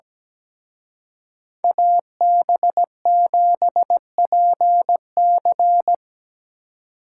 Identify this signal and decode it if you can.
Morse code, heard as a keyed tone, about 20 wpm, 700 Hz. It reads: AB7PC
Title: Speed Racing - USA 2x2 Call Signs 17wpm